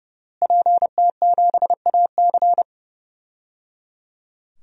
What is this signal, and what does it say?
Morse code, heard as a keyed tone, about 30 wpm, 700 Hz. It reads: PT7AC
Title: DXCC-100 Call Signs - 5 Characters - Repeated 3x - 30wpm